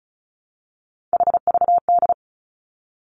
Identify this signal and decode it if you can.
Morse code, heard as a keyed tone, about 36 wpm, 700 Hz. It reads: HVD
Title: Three Challenging Letter-Number Combinations 35wpm